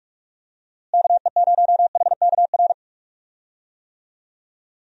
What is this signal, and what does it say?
Morse code, heard as a keyed tone, about 47 wpm, 700 Hz. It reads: KE0HKR
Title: USA Call Signs - 6 Characters - Repeated 3x - 45wpm